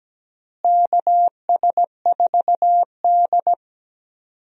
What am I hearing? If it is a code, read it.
Morse code, heard as a keyed tone, about 17 wpm, 700 Hz. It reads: KS4D
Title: USA Call Signs - 4 Characters - Repeated 3x - 17wpm